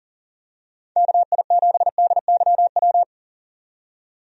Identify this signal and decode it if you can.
Morse code, heard as a keyed tone, about 40 wpm, 700 Hz. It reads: KI7DYW
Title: USA Call Signs - 6 Characters - 40wpm